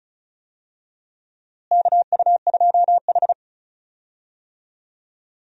Morse code, heard as a keyed tone, about 34 wpm, 700 Hz. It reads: KU2H
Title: USA Call Signs - 4 Characters - Repeated 3x - 35wpm